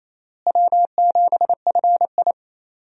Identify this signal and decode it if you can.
Morse code, heard as a keyed tone, about 28 wpm, 700 Hz. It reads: W7FI
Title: USA Call Signs - 4 Characters - Repeated 3x - 28wpm